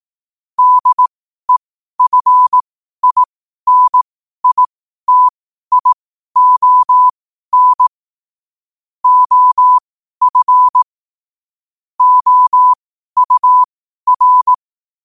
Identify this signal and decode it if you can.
Morse code, heard as a keyed tone, about 18 wpm, spaced as for 12 wpm, 1000 Hz. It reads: DEFINITION OF OUR
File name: qotdfiles / 12wpm